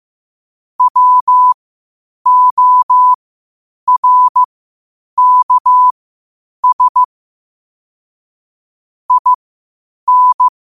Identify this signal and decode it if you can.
Morse code, heard as a keyed tone, about 15 wpm, spaced as for 8 wpm, 1000 Hz. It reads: WORKS IN
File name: qotdfiles / 8wpm